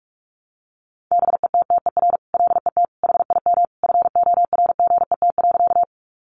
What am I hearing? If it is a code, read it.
Morse code, heard as a keyed tone, about 44 wpm, 700 Hz. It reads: BETTER LET HIM FORGET.
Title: Sentences from Top 1200 Words 45wpm